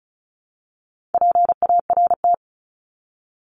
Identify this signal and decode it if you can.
Morse code, heard as a keyed tone, about 34 wpm, 700 Hz. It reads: PART
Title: ES:S Suffix - Top 100 Words with Base Words 35wpm